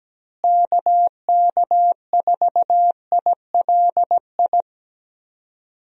Morse code, heard as a keyed tone, about 17 wpm, 700 Hz. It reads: KK4ILI
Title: USA Call Signs - 6 Characters - Repeated 3x - 17wpm